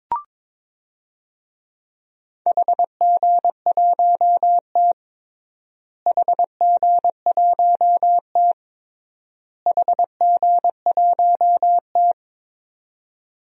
Morse code, heard as a keyed tone, about 21 wpm, 700 Hz. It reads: HG1T HG1T HG1T
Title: DXCC-100 Call Signs - 4 Characters - Repeated 3x - 22wpm